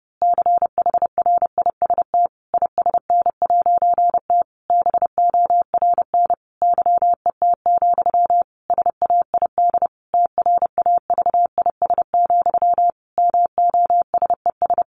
Morse code, heard as a keyed tone, about 28 wpm, 700 Hz. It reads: CHRIST ISN'T BORN YET, SAID TRAVIS, MOSES
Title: A Sound Of Thunder 30wpm